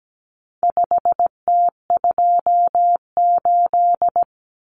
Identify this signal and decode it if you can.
Morse code, heard as a keyed tone, about 17 wpm, 700 Hz. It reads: 5T28
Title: Common Cut Numbers - 4-Digit 17wpm